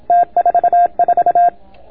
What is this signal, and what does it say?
Morse code, heard as a keyed tone, about 27 wpm, 690 Hz. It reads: T44